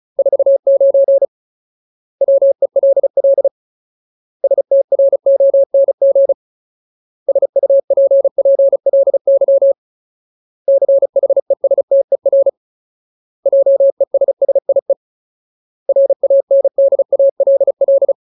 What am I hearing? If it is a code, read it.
Morse code, heard as a keyed tone, about 36 wpm, 550 Hz. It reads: 49 WELL STRONG SUPPLY CHESTER JESSIE RANDALL
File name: Ultimate Word List / 35WPM Ultimate